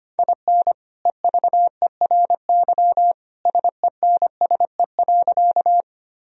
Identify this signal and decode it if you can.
Morse code, heard as a keyed tone, about 25 wpm, 700 Hz. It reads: IN EVERY SENSE.
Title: Sentences from Top 1000 Words - Review Part 4 - 25wpm